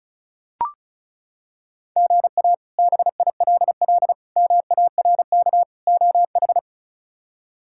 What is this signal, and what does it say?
Morse code, heard as a keyed tone, about 34 wpm, 700 Hz. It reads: GA BILL MARK OH
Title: K1USN SST QSO Practice - 35wpm